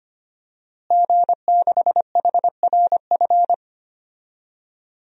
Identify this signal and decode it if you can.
Morse code, heard as a keyed tone, about 25 wpm, 700 Hz. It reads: G6HRF
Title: DXCC-100 Call Signs - 5 Characters - Repeated 2x - 25wpm